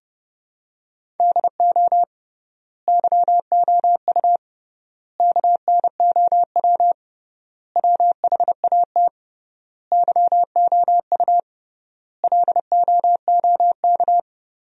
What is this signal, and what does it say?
Morse code, heard as a keyed tone, about 32 wpm, 700 Hz. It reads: DO YOU KNOW WHAT YOU LOOK
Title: 3x Word-Spacing - Sentences from Top 100 Words - 30wpm